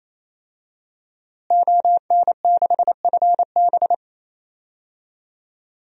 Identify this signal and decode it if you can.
Morse code, heard as a keyed tone, about 28 wpm, 700 Hz. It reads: ON6FB
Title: DXCC-100 Call Signs - 5 Characters - Repeated 3x - 28wpm